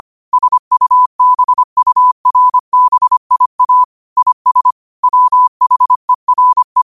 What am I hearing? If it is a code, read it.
Morse code, heard as a keyed tone, about 25 wpm, 1000 Hz. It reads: SUBURBIA IS WHERE
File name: qotdfiles / 25wpm